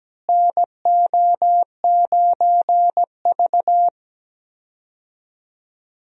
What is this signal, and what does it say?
Morse code, heard as a keyed tone, about 17 wpm, 700 Hz. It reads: NO9V
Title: USA Call Signs - 4 Characters - Repeated 2x - 17wpm